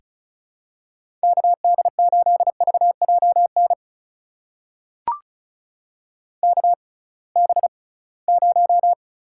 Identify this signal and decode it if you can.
Morse code, heard as a keyed tone, about 36 wpm, spaced as for 23 wpm, 690 Hz. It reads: KD8VJN KB0
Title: USA Call Signs - 6 Characters - 6x Character Spacing - 35wpm